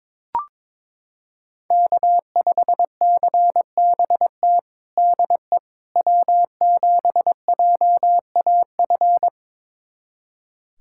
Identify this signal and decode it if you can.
Morse code, heard as a keyed tone, about 22 wpm, 700 Hz. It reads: K5CBT DE W7JAF
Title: USA Callsign DE Callsign - 5 Characters - 22wpm